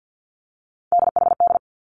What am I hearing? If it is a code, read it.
Morse code, heard as a keyed tone, about 50 wpm, 700 Hz. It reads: DHD
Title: Three Challenging Letter-Number Combinations 50wpm